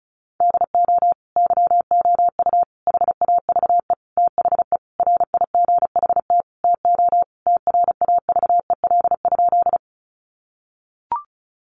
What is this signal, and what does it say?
Morse code, heard as a keyed tone, about 34 wpm, 700 Hz. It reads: DO YOU HAVE THE RIGHT TO TRAVEL?
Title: Sentences from Top 1400 Words 35wpm